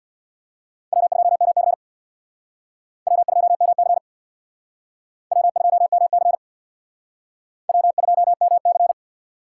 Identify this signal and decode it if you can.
Morse code, heard as a keyed tone, about 64 wpm, 700 Hz. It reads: W2MC W2MC W2MC W2MC